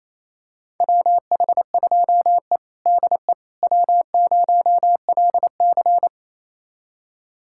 Morse code, heard as a keyed tone, about 27 wpm, 700 Hz. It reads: WH2E DE W0LC
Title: USA Callsign DE Callsign - 4 Characters - 28wpm